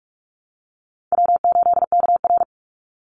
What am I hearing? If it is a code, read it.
Morse code, heard as a keyed tone, about 46 wpm, 700 Hz. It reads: W8KR
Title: Easy Call Signs - 45wpm